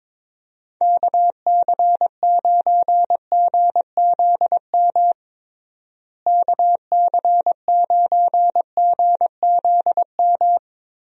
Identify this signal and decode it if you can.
Morse code, heard as a keyed tone, about 22 wpm, 700 Hz. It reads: KC9GZM KC9GZM
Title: USA Call Signs - 6 Characters - Repeated 3x - 22wpm